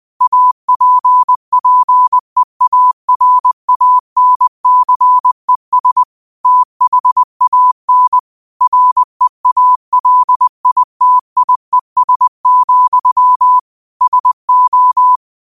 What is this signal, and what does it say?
Morse code, heard as a keyed tone, about 20 wpm, 1000 Hz. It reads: APPEARANCES THAN REALITIES, SO
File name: qotdfiles / 20wpm